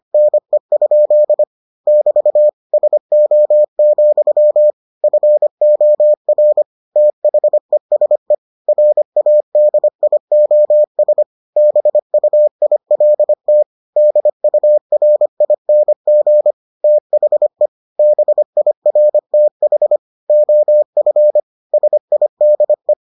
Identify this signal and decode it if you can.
Morse code, heard as a keyed tone, about 25 wpm, 600 Hz. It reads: NE? <BT> SO, FOR THESE RADIOS BUILT DURING THE BIRTH OF SIDE